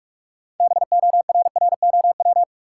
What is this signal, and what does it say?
Morse code, heard as a keyed tone, about 44 wpm, 700 Hz. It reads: BORROW
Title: Sentences from Top 1500 Words - New Word Drill Alt - 45wpm